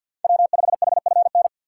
Morse code, heard as a keyed tone, about 51 wpm, 700 Hz. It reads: W5HFN
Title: USA Call Signs - 3x Character Spacing - 50wpm